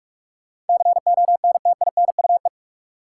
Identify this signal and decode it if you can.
Morse code, heard as a keyed tone, about 46 wpm, 700 Hz. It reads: CONTINUE
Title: Sentences from Top 1000 Words - Review Part 6 - New Word Drill - 45wpm